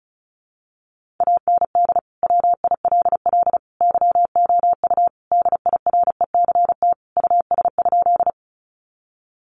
Morse code, heard as a keyed tone, about 34 wpm, 700 Hz. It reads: AND WILL YOU DIRECT US?
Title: Sentences from Top 400 Words 35wpm